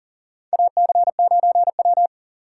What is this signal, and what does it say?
Morse code, heard as a keyed tone, about 41 wpm, 700 Hz. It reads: AC9W
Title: USA Call Signs - 6x Character Spacing - 40wpm